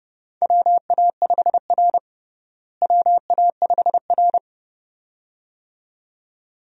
Morse code, heard as a keyed tone, about 30 wpm, 700 Hz. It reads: WA5R WA5R
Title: USA Call Signs - 4 Characters - Repeated 2x - 30wpm